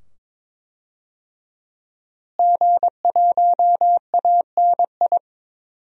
Morse code, heard as a keyed tone, about 22 wpm, 700 Hz. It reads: G1ANI
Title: DXCC-100 Call Signs - 5 Characters - Repeated 2x - 22wpm